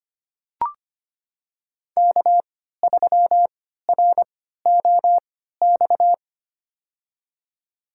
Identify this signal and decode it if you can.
Morse code, heard as a keyed tone, about 25 wpm, spaced as for 14 wpm, 700 Hz. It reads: K3ROX
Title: USA Call Signs - 3x Character Spacing - 25wpm